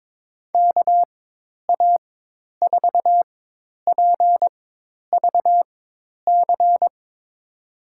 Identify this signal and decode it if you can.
Morse code, heard as a keyed tone, about 22 wpm, spaced as for 10 wpm, 700 Hz. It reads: KA4PVC
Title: USA Call Signs - 6 Characters - 4x Character Spacing - 22wpm